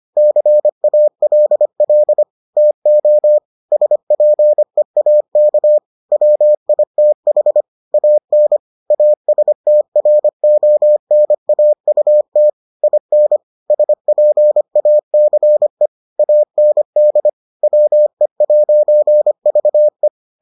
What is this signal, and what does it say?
Morse code, heard as a keyed tone, about 25 wpm, 600 Hz. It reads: CALL TO SPEAK WITH AN ASTRONAUT IN SPACE AND WE'VE